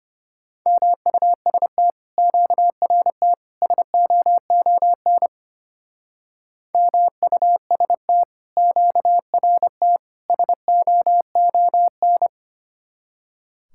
Morse code, heard as a keyed tone, about 30 wpm, 700 Hz. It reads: MUST QRT SOON MUST QRT SOON
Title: Speed-Racing Common QSO Phrases - 25wpm